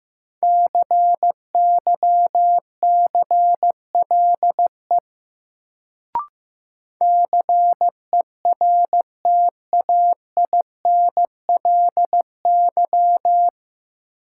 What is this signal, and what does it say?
Morse code, heard as a keyed tone, about 15 wpm, 700 Hz. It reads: CYCLE CERTAINLY
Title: Sentences from Top 900 Words - New Word Drill - 15wpm